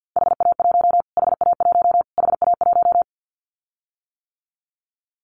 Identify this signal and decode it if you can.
Morse code, heard as a keyed tone, about 50 wpm, 700 Hz. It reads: HA1 HA1 HA1
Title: Callsign Prefixes for Top 25 DXCC - Repeated 3x - 50wpm